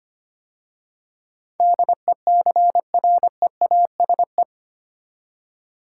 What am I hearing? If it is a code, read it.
Morse code, heard as a keyed tone, about 25 wpm, 700 Hz. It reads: DECREASE
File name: 8 Letter Words - First 3 Letters Given - 25wpm